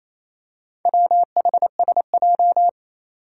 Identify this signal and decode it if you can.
Morse code, heard as a keyed tone, about 28 wpm, 700 Hz. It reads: WHSJ
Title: Four Challenging Letter-Number Combinations 28wpm